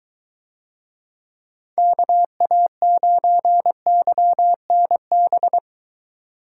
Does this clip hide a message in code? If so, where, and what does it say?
Yes: Morse code from 1.8 to 5.6 s, heard as a keyed tone, about 21 wpm, 700 Hz. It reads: KA9YNB